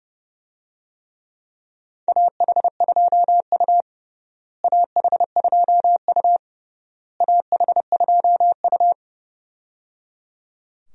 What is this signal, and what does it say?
Morse code, heard as a keyed tone, about 30 wpm, 700 Hz. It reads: AH2U AH2U AH2U
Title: USA Call Signs - 4 Characters - Repeated 3x - 30wpm